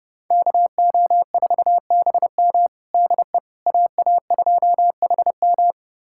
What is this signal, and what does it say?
Morse code, heard as a keyed tone, about 31 wpm, 700 Hz. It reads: KO4BM DE AA2HM
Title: USA Callsign DE Callsign - 5 Characters - 30wpm